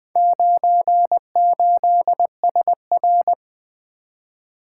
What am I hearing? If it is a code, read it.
Morse code, heard as a keyed tone, about 24 wpm, 700 Hz. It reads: 98SR
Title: Speed-Racing - Four Letter-Number Combinations - 20wpm